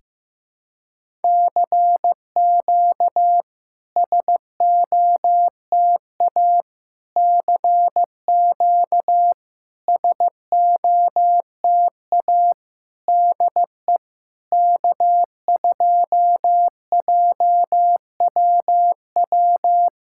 Morse code, heard as a keyed tone, about 15 wpm, 700 Hz. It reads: CQ SOTA CQ SOTA DE K2JWW
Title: SOTA QSO Practice 15wpm